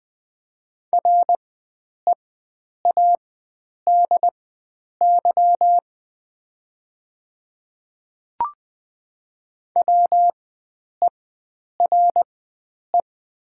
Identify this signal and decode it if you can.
Morse code, heard as a keyed tone, about 20 wpm, spaced as for 9 wpm, 700 Hz. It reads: READY WERE
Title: 4x Character Spacing - Top 500 Words - 20wpm